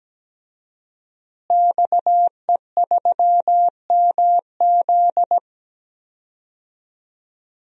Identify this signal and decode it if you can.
Morse code, heard as a keyed tone, about 17 wpm, 700 Hz. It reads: XE3MZ
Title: DXCC-100 Call Signs - 5 Characters - Repeated 2x - 17wpm